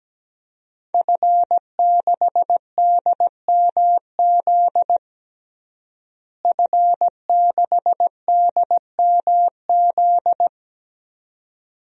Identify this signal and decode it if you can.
Morse code, heard as a keyed tone, about 17 wpm, 700 Hz. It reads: F6DMZ F6DMZ
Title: DXCC-100 Call Signs - 5 Characters - Repeated 2x - 17wpm